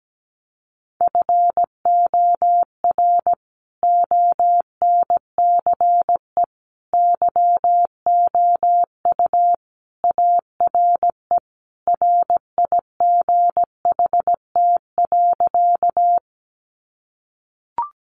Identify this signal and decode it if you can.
Morse code, heard as a keyed tone, about 17 wpm, 700 Hz. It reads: FOR ONCE YOU ARE RIGHT.
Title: Sentences from Top 300 Words 17wpm